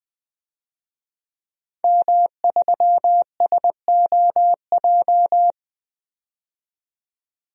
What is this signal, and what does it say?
Morse code, heard as a keyed tone, about 20 wpm, 700 Hz. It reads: M3SOJ
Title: DXCC-100 Call Signs - 5 Characters - 20wpm